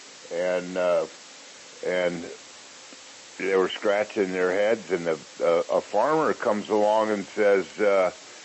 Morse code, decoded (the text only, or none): none